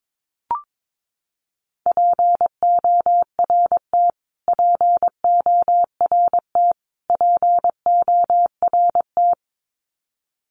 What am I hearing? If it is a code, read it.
Morse code, heard as a keyed tone, about 22 wpm, 700 Hz. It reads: PORT PORT PORT
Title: Top 200 Words Repeated 22wpm